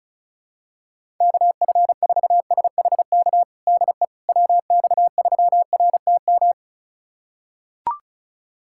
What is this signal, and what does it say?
Morse code, heard as a keyed tone, about 36 wpm, 700 Hz. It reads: KF4SHK DE WX3RTM
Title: USA Callsign DE Callsign - 6 Characters - 35wpm